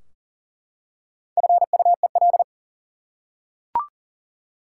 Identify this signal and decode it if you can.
Morse code, heard as a keyed tone, about 40 wpm, 700 Hz. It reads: FUEL
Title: Four Letter Words - 40wpm